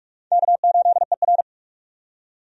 Morse code, heard as a keyed tone, about 46 wpm, 700 Hz. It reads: K8ER
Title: USA Call Signs - 4 Characters - 45wpm